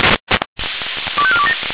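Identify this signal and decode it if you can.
Five-tone selective call (ZVEI), heard as quick runs of five tones, tones of about 70 ms. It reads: 25427
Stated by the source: ZVEI five-tone selective call, code 25304, received by SDR.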